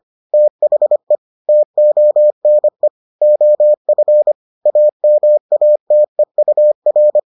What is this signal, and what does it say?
Morse code, heard as a keyed tone, about 25 wpm, 600 Hz. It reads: THE TONE OF AMATEUR